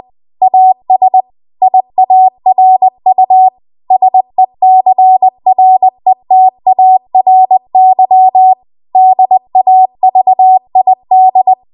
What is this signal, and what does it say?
Morse code, heard as a keyed tone, about 20 wpm, 750 Hz. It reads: AS IARU SECRETARY DAVID
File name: Archive / 20 WPM / 20WPM